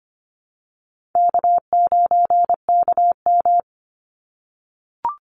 Morse code, heard as a keyed tone, about 25 wpm, 700 Hz. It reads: K9KM
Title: Call Signs - 25wpm